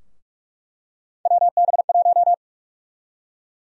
Morse code, heard as a keyed tone, about 43 wpm, 700 Hz. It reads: WB1
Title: USA Call Sign Prefixes - Repeated 2x - 45wpm